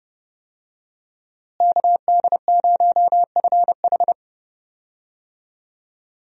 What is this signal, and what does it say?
Morse code, heard as a keyed tone, about 29 wpm, 700 Hz. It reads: KD0FH
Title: USA Call Signs - 5 Characters - Repeated 2x - 30wpm